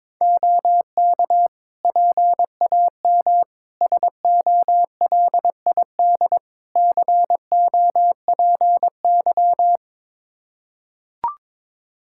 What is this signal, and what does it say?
Morse code, heard as a keyed tone, about 22 wpm, 700 Hz. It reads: OK PAM SOLID COPY
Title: Practice QSO - Exchange 3 - More Info - 22wpm